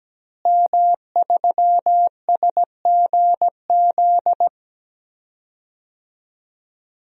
Morse code, heard as a keyed tone, about 17 wpm, 700 Hz. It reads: M3SGZ
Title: DXCC-100 Call Signs - 5 Characters - Repeated 2x - 17wpm